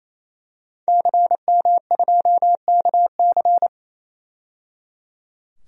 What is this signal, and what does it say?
Morse code, heard as a keyed tone, about 28 wpm, 700 Hz. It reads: CM2KC
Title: DXCC-100 Call Signs - 5 Characters - 28wpm